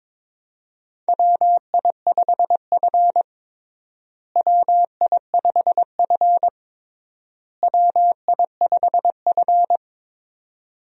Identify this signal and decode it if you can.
Morse code, heard as a keyed tone, about 22 wpm, 700 Hz. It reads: WI5F WI5F WI5F
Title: USA Call Signs - 4 Characters - Repeated 3x - 22wpm